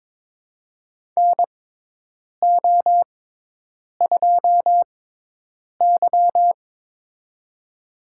Morse code, heard as a keyed tone, about 23 wpm, spaced as for 8 wpm, 700 Hz. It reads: NO2Y
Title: USA Call Signs - 6x Character Spacing - 22wpm